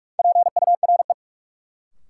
Morse code, heard as a keyed tone, about 46 wpm, 700 Hz. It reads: PURE
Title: Sentences from Top 1500 Words - New Word Drill - 45wpm